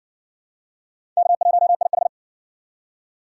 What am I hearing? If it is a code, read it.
Morse code, heard as a keyed tone, about 58 wpm, 700 Hz. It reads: K1IH